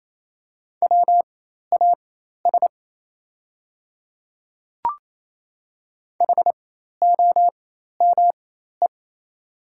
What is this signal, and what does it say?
Morse code, heard as a keyed tone, about 27 wpm, spaced as for 13 wpm, 700 Hz. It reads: WAS HOME
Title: 4x Character Spacing - Top 300 Words - 28wpm